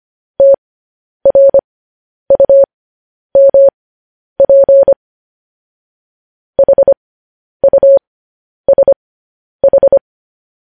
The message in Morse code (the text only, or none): TRUMP HUSH